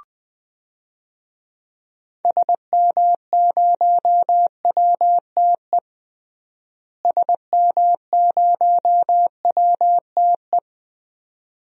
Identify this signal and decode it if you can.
Morse code, heard as a keyed tone, about 21 wpm, 700 Hz. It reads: SM0WTE SM0WTE
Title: DXCC-100 Call Signs - 6 Characters - Repeated 2x - 20wpm